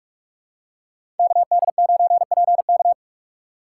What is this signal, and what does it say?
Morse code, heard as a keyed tone, about 46 wpm, 700 Hz. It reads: KD9PK
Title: USA Call Signs - 5 Characters - 3x Character Spacing - 45wpm